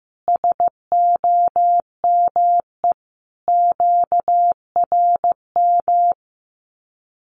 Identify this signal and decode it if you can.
Morse code, heard as a keyed tone, about 14 wpm, 700 Hz. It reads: SOME QRM
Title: Common QSO Phrases 15wpm